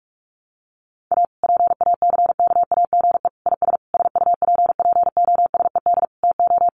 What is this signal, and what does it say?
Morse code, heard as a keyed tone, about 44 wpm, 700 Hz. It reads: A PACKAGE IS SUPPOSED TO
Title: Additional Easy Phrases 45wpm